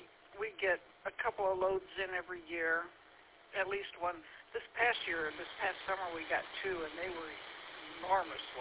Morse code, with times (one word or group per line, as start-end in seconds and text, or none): none